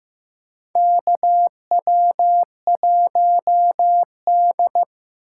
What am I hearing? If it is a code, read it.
Morse code, heard as a keyed tone, about 15 wpm, 700 Hz. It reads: KW1D
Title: USA Call Signs - 6x Character Spacing - 15wpm